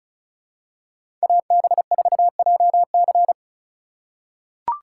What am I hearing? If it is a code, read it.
Morse code, heard as a keyed tone, about 36 wpm, 700 Hz. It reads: AB4JC
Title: USA Call Signs - 3x Character Spacing - 35wpm